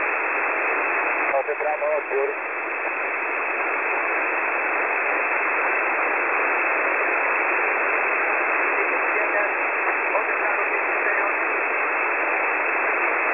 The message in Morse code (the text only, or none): none